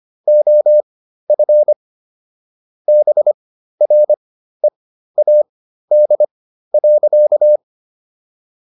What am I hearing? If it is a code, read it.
Morse code, heard as a keyed tone, about 25 wpm, spaced as for 13 wpm, 600 Hz. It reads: OF BREAD.
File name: SS109 13